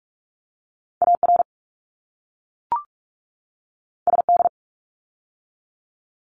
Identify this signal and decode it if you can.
Morse code, heard as a keyed tone, about 44 wpm, 700 Hz. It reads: AR SD
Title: ARRL Sections 45wpm